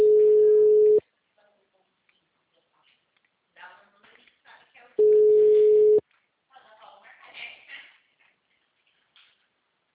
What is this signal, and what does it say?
Ringback tone (Europe, 425 Hz), with speech.